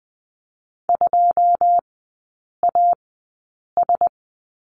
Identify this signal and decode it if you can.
Morse code, heard as a keyed tone, about 20 wpm, spaced as for 8 wpm, 690 Hz. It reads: 2AS
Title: 087 Lesson 15e - Number 5 - Three Characters 20wpm